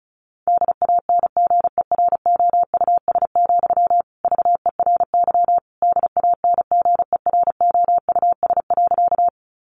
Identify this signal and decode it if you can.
Morse code, heard as a keyed tone, about 34 wpm, 700 Hz. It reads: DANGEROUS, VERY DANGEROUS.
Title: Sentences from Top 1200 Words 35wpm